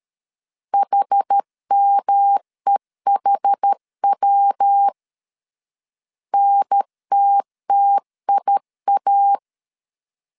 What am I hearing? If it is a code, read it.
Morse code, heard as a keyed tone, about 13 wpm, 800 Hz. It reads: HMEHW NTTIA
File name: CW / DET / 1 3 1500C S A 12wpm 2905